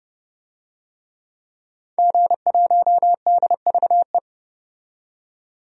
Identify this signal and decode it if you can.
Morse code, heard as a keyed tone, about 30 wpm, 690 Hz. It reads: G1DVE
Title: DXCC-100 Call Signs - 5 Characters - 30wpm